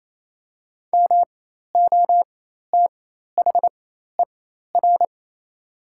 Morse code, heard as a keyed tone, about 28 wpm, spaced as for 13 wpm, 700 Hz. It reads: MOTHER
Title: 4x Character Spacing - Top 300 Words - 28wpm